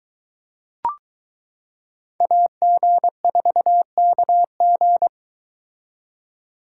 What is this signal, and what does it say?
Morse code, heard as a keyed tone, about 22 wpm, 700 Hz. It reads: AG4KG